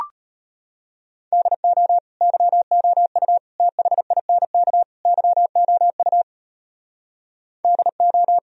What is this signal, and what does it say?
Morse code, heard as a keyed tone, about 33 wpm, 700 Hz. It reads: DO YOU THINK YOU DO